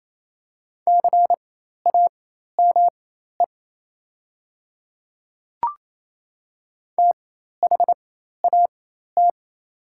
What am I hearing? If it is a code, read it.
Morse code, heard as a keyed tone, about 28 wpm, spaced as for 13 wpm, 700 Hz. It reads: CAME THAT
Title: 4x Character Spacing - Top 300 Words - 28wpm